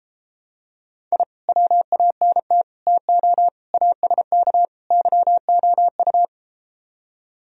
Morse code, heard as a keyed tone, about 33 wpm, 700 Hz. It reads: I WANT TO ASK YOU